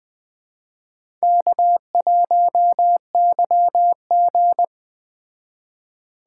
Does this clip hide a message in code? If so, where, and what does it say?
Yes: Morse code from 1.2 to 4.6 s, heard as a keyed tone, about 20 wpm, 700 Hz. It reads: K1YG